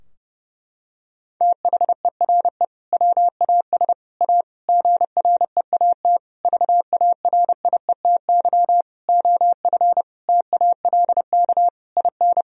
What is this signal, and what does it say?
Morse code, heard as a keyed tone, about 32 wpm, 700 Hz. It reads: THERE WAS A GREAT VARIETY OF TALK IN
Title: Sentences from Top 1000 Words - Review Part 4 - 30wpm